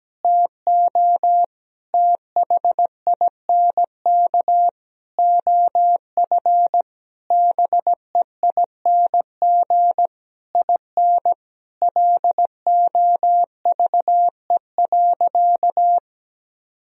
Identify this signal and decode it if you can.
Morse code, heard as a keyed tone, about 17 wpm, 700 Hz. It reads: TO THINK OF BEING IN LOVE.
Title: Sentences from Top 1000 Words - Review Part 3 - 17wpm